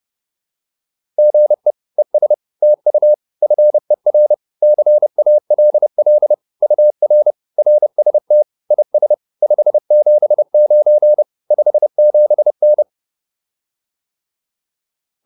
Morse code, heard as a keyed tone, about 30 wpm, 600 Hz. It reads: GE ES TU FER CALL UR RST IS 579 57N
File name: QSO101 30